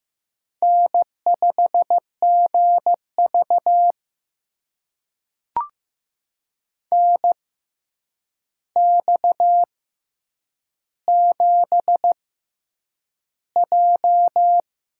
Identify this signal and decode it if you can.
Morse code, heard as a keyed tone, about 15 wpm, spaced as for 8 wpm, 700 Hz. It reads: N5GV NX7J
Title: USA Call Signs - 6x Character Spacing - 15wpm